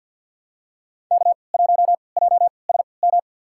Morse code, heard as a keyed tone, about 49 wpm, spaced as for 28 wpm, 700 Hz. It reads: K1JSM
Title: USA Call Signs - 3x Character Spacing - 50wpm